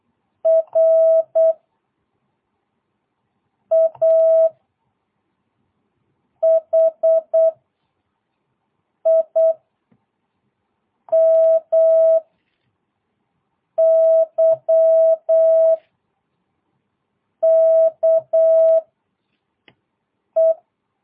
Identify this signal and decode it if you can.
Morse code, heard as a keyed tone, about 8 wpm, 660 Hz. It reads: R A H I M Y K E
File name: jh8cbh / 2022cw